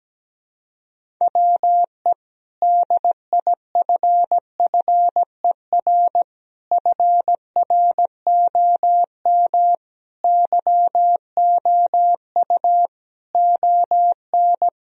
Morse code, heard as a keyed tone, about 17 wpm, 700 Hz. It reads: WE DIFFER FROM YOU ON
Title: Sentences from Top 1000 Words - Review Part 1 - 17wpm